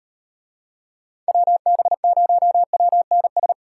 Morse code, heard as a keyed tone, about 38 wpm, 700 Hz. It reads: WB0WNS